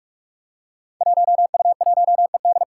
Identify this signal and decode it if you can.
Morse code, heard as a keyed tone, about 42 wpm, 700 Hz. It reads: 1U1ED
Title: Five Challenging Letter-Number Combinations 45wpm